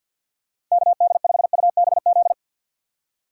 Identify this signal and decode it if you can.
Morse code, heard as a keyed tone, about 50 wpm, 690 Hz. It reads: KD5UBZ
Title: USA Call Signs - 6 Characters - 8x Character Spacing - 50wpm